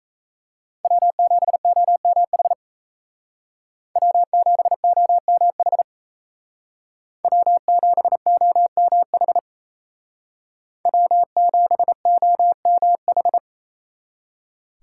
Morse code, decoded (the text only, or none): W7OMH W7OMH W7OMH W7OMH